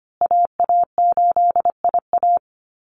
Morse code, heard as a keyed tone, about 25 wpm, 700 Hz. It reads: AA8IA
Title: Call Signs Repeated 2x - 25wpm